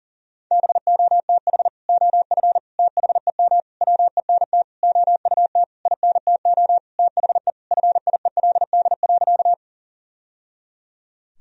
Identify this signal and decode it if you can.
Morse code, heard as a keyed tone, about 40 wpm, 700 Hz. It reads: BOTH OF THEM WENT OUT INTO THE FIELD.
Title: Sentences from Top 1000 Words - Review Part 2 - 40wpm